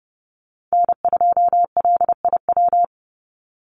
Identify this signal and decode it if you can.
Morse code, heard as a keyed tone, about 30 wpm, 700 Hz. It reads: N2LIW
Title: USA Call Signs 30wpm